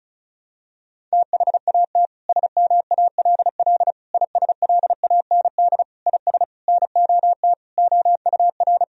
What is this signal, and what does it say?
Morse code, heard as a keyed tone, about 36 wpm, 700 Hz. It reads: THAT SMALL ISLAND IS NOT OUR
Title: Sentences from Top 1000 Words - Review Part 4 - 35wpm